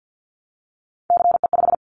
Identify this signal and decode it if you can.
Morse code, heard as a keyed tone, about 50 wpm, 700 Hz. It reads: CE5
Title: Callsign Prefixes for Top 100 DXCC - 50wpm